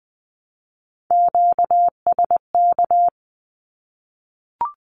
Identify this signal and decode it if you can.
Morse code, heard as a keyed tone, about 20 wpm, 700 Hz. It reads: QSK
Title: Q-Codes Repeated 3x 20wpm